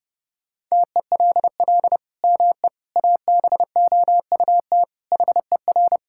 Morse code, heard as a keyed tone, about 30 wpm, 700 Hz. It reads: TELL ME ABOUT HER
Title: Sentences from Top 1000 Words - Review Part 5 - 30wpm